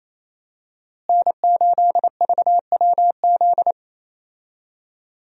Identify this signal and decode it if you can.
Morse code, heard as a keyed tone, about 27 wpm, 700 Hz. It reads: N8VWZ
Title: USA Call Signs - 5 Characters - Repeated 3x - 28wpm